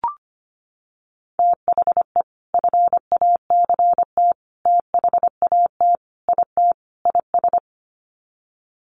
Morse code, heard as a keyed tone, about 25 wpm, 700 Hz. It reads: THE FACT THAT IT IS
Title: Sets of 5 Words Encore2 25wpm